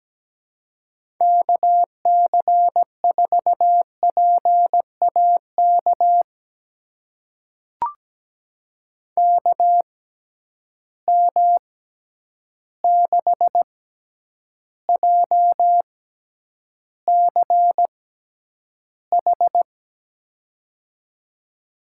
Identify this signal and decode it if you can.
Morse code, heard as a keyed tone, about 17 wpm, spaced as for 9 wpm, 700 Hz. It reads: KC4PAK KM6JCH
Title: USA Call Signs - 6 Characters - 6x Character Spacing - 17wpm